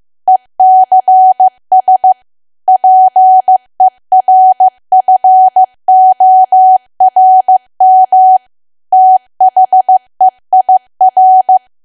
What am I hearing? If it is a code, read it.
Morse code, heard as a keyed tone, about 15 wpm, 750 Hz. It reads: ECS PERFORM THEIR
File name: cwa / W1AW-Level2 / 15WPM